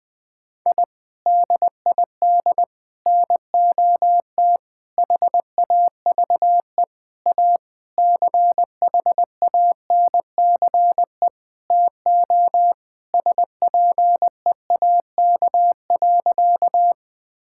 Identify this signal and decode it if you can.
Morse code, heard as a keyed tone, about 21 wpm, 700 Hz. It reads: I DID NOT HAVE A CHANCE TO SPEAK.
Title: Sentences with 5-Word Phrases - 20wpm